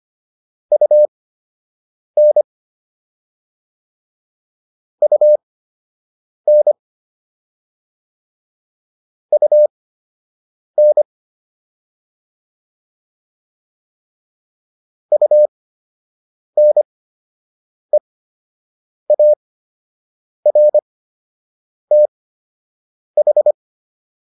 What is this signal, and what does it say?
Morse code, heard as a keyed tone, about 25 wpm, spaced as for 7 wpm, 600 Hz. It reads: UN UN UN UNEARTH